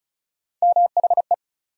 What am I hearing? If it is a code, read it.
Morse code, heard as a keyed tone, about 36 wpm, 700 Hz. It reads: MHE
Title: USA Call Sign Postfixes - 3 Letters 35wpm